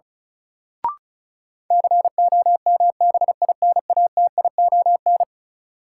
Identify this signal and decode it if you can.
Morse code, heard as a keyed tone, about 36 wpm, 700 Hz. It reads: COMBINATION
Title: Sentences from Top 1000 Words - Review Part 6 - New Word Drill - 35wpm